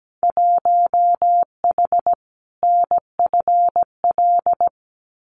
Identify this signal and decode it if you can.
Morse code, heard as a keyed tone, about 17 wpm, 700 Hz. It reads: 1H NFL
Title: Winter Field Day Exchanges 17wpm